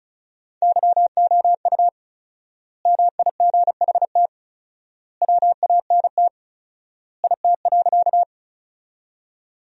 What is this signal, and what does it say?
Morse code, heard as a keyed tone, about 36 wpm, 700 Hz. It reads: YOU MIGHT WANT IT.
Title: 4x Word-Spacing - Sentences from Top 300 Words - 35wpm